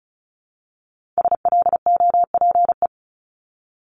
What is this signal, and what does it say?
Morse code, heard as a keyed tone, about 36 wpm, 700 Hz. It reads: SLOPE
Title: Multi-letter Phonemes Repeated 3x 35wpm